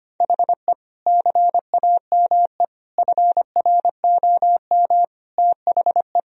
Morse code, heard as a keyed tone, about 25 wpm, 700 Hz. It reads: HE CAME FROM THE
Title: Sentences from Top 1000 Words - Review Part 3 - 25wpm